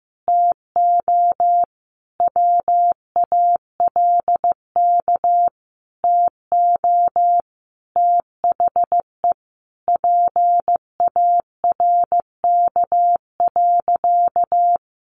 Morse code, heard as a keyed tone, about 15 wpm, 700 Hz. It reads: TO WALK TO THE PARK.
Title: Sentences from Top 1400 Words 15wpm